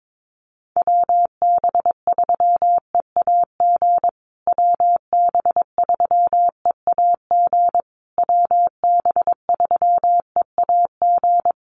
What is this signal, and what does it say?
Morse code, heard as a keyed tone, about 22 wpm, 700 Hz. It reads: WB3EAG WB3EAG WB3EAG
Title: USA Call Signs Repeated 3x 22wpm